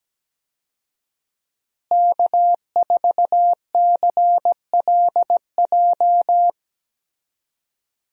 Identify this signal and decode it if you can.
Morse code, heard as a keyed tone, about 17 wpm, 700 Hz. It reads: K4CLJ
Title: USA Call Signs - 5 Characters - Repeated 3x - 17wpm